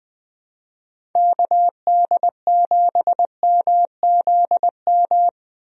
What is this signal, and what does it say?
Morse code, heard as a keyed tone, about 20 wpm, 700 Hz. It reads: KD7MZM
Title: USA Call Signs - 6 Characters - 20wpm